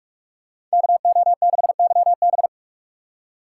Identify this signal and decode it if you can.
Morse code, heard as a keyed tone, about 41 wpm, 700 Hz. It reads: KO6YB